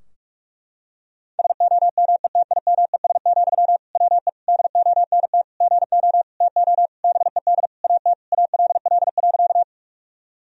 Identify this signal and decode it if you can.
Morse code, heard as a keyed tone, about 43 wpm, 700 Hz. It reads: SOMETIMES, WE DONT GO TO BED AT ALL.
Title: Sentences from Top 1000 Words - Review Part 3 - 45wpm